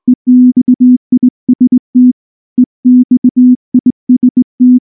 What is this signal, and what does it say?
Morse code, heard as a keyed tone, about 20 wpm, 250 Hz. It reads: EXIST EXIST
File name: morse / morse-training-20WPM-10